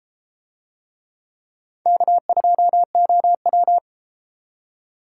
Morse code, heard as a keyed tone, about 31 wpm, 700 Hz. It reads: K2OW